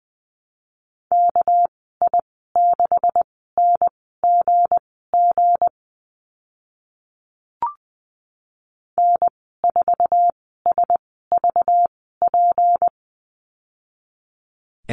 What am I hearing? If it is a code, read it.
Morse code, heard as a keyed tone, about 20 wpm, spaced as for 14 wpm, 700 Hz. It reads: KI6NGG N4SVP
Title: 213 Lesson 34g - Number 8 - Call Signs 20wpm